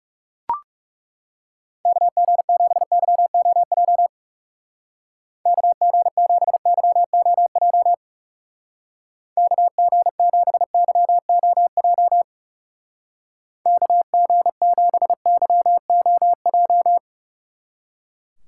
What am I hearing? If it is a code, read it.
Morse code, heard as a keyed tone, about 37 wpm, 700 Hz. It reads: KG7YOJ KG7YOJ KG7YOJ KG7YOJ
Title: Speed Racing - USA 2x3 Call Signs 30wpm